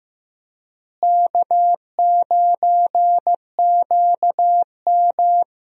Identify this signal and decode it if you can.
Morse code, heard as a keyed tone, about 15 wpm, 700 Hz. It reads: K9QM
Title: USA Call Signs - 4 Characters - Repeated 2x - 15wpm